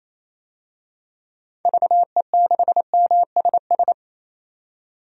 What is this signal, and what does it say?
Morse code, heard as a keyed tone, about 28 wpm, 700 Hz. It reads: VE6MSS
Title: DXCC-100 Call Signs - 6 Characters - 28wpm